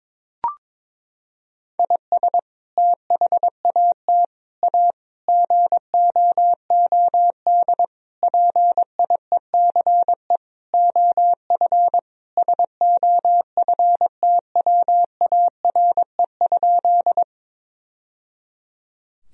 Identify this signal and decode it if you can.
Morse code, heard as a keyed tone, about 22 wpm, 700 Hz. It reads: IS THAT A GOOD PIECE OF SOFTWARE?
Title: Sentences from Top 1000 Words - Review Part 3 - 22wpm